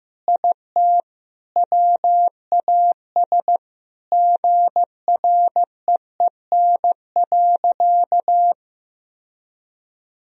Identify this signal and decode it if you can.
Morse code, heard as a keyed tone, about 15 wpm, 700 Hz. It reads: IT WAS GREEN.
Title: Sentences from Top 1000 Words - Review Part 2 - 15wpm